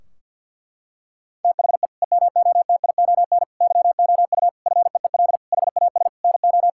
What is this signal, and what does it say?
Morse code, heard as a keyed tone, about 50 wpm, 700 Hz. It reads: THE EMOTION YOU FEEL HAS NO
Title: Sentences from Top 1600 Words 50wpm